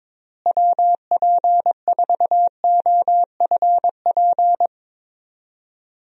Morse code, heard as a keyed tone, about 22 wpm, 700 Hz. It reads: WP4OFP
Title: USA Call Signs - 6 Characters - Repeated 2x - 22wpm